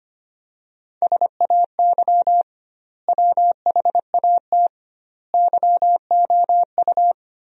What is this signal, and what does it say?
Morse code, heard as a keyed tone, about 25 wpm, 700 Hz. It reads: SAY WHAT YOU
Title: 2x Word-Spacing - Sentences from Top 200 Words - 25wpm